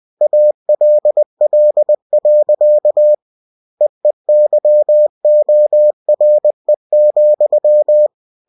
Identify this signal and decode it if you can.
Morse code, heard as a keyed tone, about 20 wpm, 600 Hz. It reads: ALL. EEYORE,
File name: cw-books / winnie-the-pooh / winnie-20wpm-page-0081